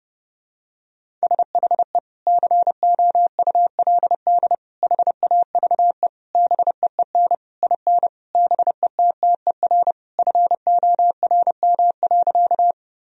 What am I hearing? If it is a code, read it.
Morse code, heard as a keyed tone, about 30 wpm, 700 Hz. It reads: SHE COULD HAVE BEEN IN BETTER FORM.
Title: Sentences from Top 1000 Words - Review Part 1 - 30wpm